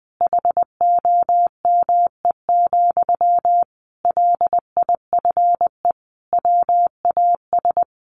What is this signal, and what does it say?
Morse code, heard as a keyed tone, about 20 wpm, 700 Hz. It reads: HOME, LIFE WAS
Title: Sentences from Top 1400 Words 20wpm